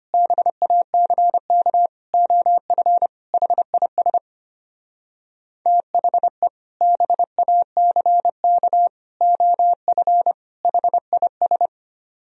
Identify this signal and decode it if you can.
Morse code, heard as a keyed tone, about 33 wpm, 700 Hz. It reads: BACK OF HIS THE BACK OF HIS